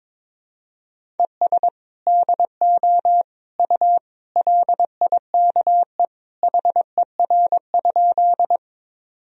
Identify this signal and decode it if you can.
Morse code, heard as a keyed tone, about 22 wpm, 700 Hz. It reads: ES DO U LIKE HER?
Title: Sentences from Top 100 Words and Abbreviations 22wpm